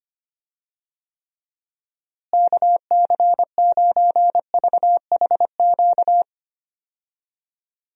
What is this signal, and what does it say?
Morse code, heard as a keyed tone, about 25 wpm, 700 Hz. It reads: KC9VHQ
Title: USA Call Signs - 6 Characters - Repeated 2x - 25wpm